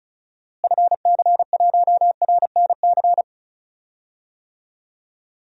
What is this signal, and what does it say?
Morse code, heard as a keyed tone, about 34 wpm, 700 Hz. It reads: FC1RNC
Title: DXCC-100 Call Signs - 6 Characters - Repeated 3x - 35wpm